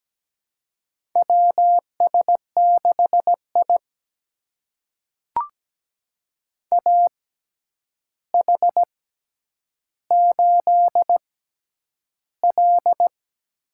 Morse code, heard as a keyed tone, about 17 wpm, spaced as for 9 wpm, 700 Hz. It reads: WS6I AH8L
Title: USA Call Signs - 4 Characters - 6x Character Spacing - 17wpm